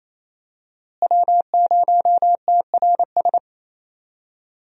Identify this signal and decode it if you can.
Morse code, heard as a keyed tone, about 28 wpm, 700 Hz. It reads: W0TRS
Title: USA Call Signs - 5 Characters - Repeated 2x - 28wpm